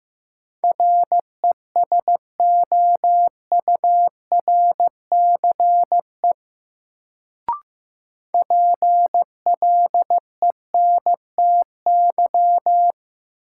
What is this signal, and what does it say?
Morse code, heard as a keyed tone, about 15 wpm, 700 Hz. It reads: RESOURCE PLENTY
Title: Sentences from Top 1300 Words - New Word Drill Alt - 15wpm